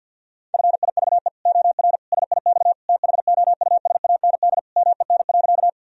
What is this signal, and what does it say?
Morse code, heard as a keyed tone, about 48 wpm, 700 Hz. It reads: FIVE OR SIX THOUSAND MEN.
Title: Sentences from Top 1000 Words - Review Part 3 - 50wpm